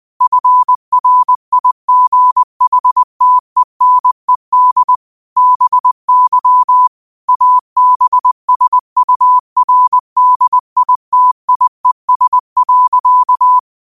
Morse code, heard as a keyed tone, about 20 wpm, 1000 Hz. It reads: FRIGHTENED BY ABSURDITIES.